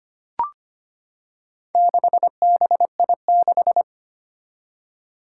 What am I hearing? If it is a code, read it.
Morse code, heard as a keyed tone, about 25 wpm, 700 Hz. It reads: 6BI6
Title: Four Challenging Letter-Number Combinations 25wpm